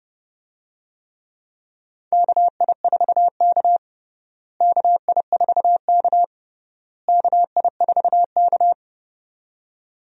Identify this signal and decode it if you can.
Morse code, heard as a keyed tone, about 30 wpm, 700 Hz. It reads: KI4K KI4K KI4K
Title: USA Call Signs - 4 Characters - Repeated 3x - 30wpm